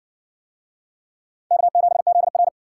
Morse code, heard as a keyed tone, about 52 wpm, 700 Hz. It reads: K7ZR